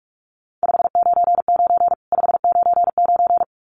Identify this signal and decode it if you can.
Morse code, heard as a keyed tone, about 44 wpm, 700 Hz. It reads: 599 599
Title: Common QSO Phrases 45wpm